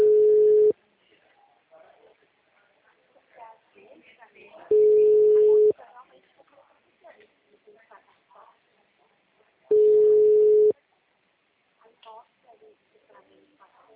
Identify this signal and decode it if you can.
Ringback tone (Europe, 425 Hz), with speech.
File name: out- -2009- - - .62730